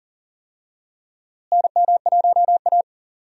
Morse code, heard as a keyed tone, about 40 wpm, 700 Hz. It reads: NM1A
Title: USA Call Signs - 4 Characters - Repeated 2x - 40wpm